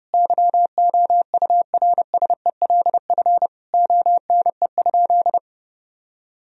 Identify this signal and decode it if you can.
Morse code, heard as a keyed tone, about 30 wpm, 700 Hz. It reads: YOURSELF ONE?
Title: Sentences from Top 1000 Words - Review Part 4 - 30wpm